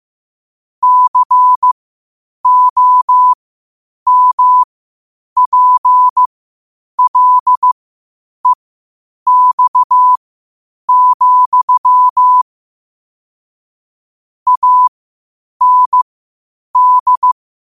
Morse code, heard as a keyed tone, about 15 wpm, spaced as for 8 wpm, 1000 Hz. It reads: COMPLEX, AND
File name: qotdfiles / 8wpm